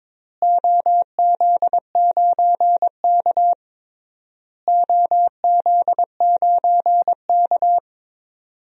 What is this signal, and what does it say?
Morse code, heard as a keyed tone, about 22 wpm, 700 Hz. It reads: OZ9K OZ9K
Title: DXCC-100 Call Signs - 4 Characters - Repeated 3x - 22wpm